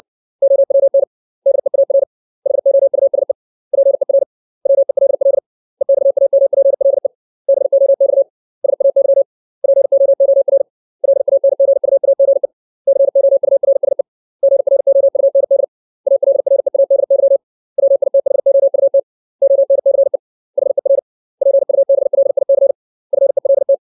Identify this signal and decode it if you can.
Morse code, heard as a keyed tone, about 60 wpm, 550 Hz. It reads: OWN DEAR HOUSE PER WELL EXAMPLE BOX SAY POOR LANGUAGE COURSE GROUND ALREADY WITHOUT ONCE HER PUBLIC FELT
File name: 632 words / 632 Words 60WPM